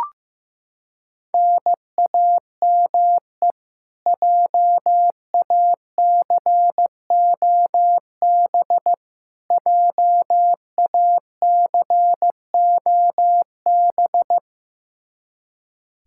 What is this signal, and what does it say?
Morse code, heard as a keyed tone, about 15 wpm, 700 Hz. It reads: NAME JACOB JACOB
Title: K1USN SST QSO Practice 15wpm